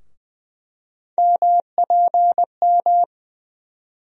Morse code, heard as a keyed tone, about 20 wpm, 700 Hz. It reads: MPM
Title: USA Call Sign Postfixes - 3 Letters 20wpm